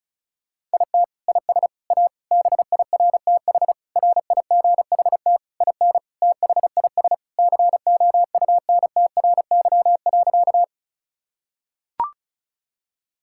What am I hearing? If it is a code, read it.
Morse code, heard as a keyed tone, about 36 wpm, 700 Hz. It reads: IT IS A BIRTH RIGHT IN THIS COUNTRY.
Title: Sentences from Top 1000 Words - Review Part 5 - 35wpm